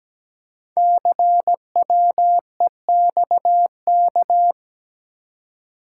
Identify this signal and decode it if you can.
Morse code, heard as a keyed tone, about 17 wpm, 700 Hz. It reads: CWEXK
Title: Code Groups - 5 Letters - 17wpm